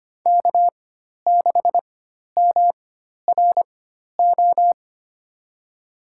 Morse code, heard as a keyed tone, about 25 wpm, spaced as for 12 wpm, 700 Hz. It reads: K6MRO
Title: USA Call Signs - 5 Characters - 4x Character Spacing - 25wpm